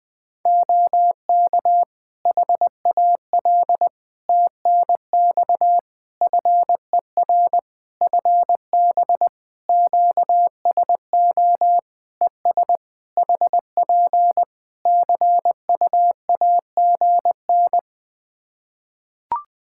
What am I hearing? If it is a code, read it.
Morse code, heard as a keyed tone, about 21 wpm, 700 Hz. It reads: OK HAL TNX FER FB QSO ES HP CU AGN
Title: Practice QSO - Exchange 4 - Ending - 20wpm